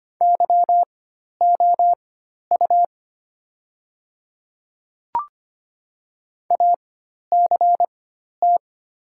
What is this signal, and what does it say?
Morse code, heard as a keyed tone, about 25 wpm, spaced as for 12 wpm, 700 Hz. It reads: YOU ACT
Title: 4x Character Spacing - Top 200 Words - 25wpm